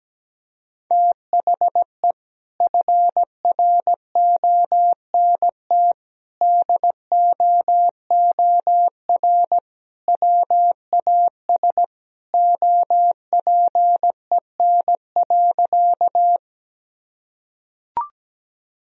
Morse code, heard as a keyed tone, about 17 wpm, 700 Hz. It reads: THE FRONT DOOR WAS OPEN.
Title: Sentences from Top 1000 Words - Review Part 2 - 17wpm